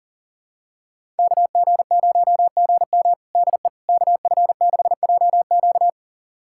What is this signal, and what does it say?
Morse code, heard as a keyed tone, about 40 wpm, 700 Hz. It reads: KG0GM DE KF6JQ
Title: USA Callsign DE Callsign - 5 Characters - 40wpm